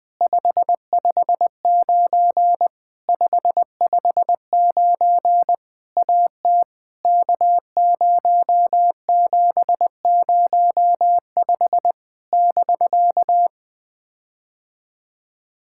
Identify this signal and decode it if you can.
Morse code, heard as a keyed tone, about 20 wpm, 700 Hz. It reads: 559 559 AT K0705 <BK>
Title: POTA QSO Practice - 20wpm